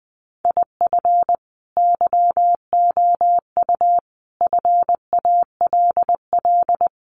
Morse code, heard as a keyed tone, about 20 wpm, 700 Hz. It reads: IF YOU FALL
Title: Sentences from Top 500 Words 20wpm